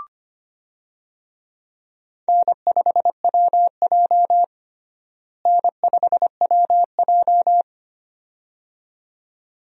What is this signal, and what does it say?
Morse code, heard as a keyed tone, about 25 wpm, 700 Hz. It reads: N5WJ N5WJ
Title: USA Call Signs - 4 Characters - Repeated 2x - 25wpm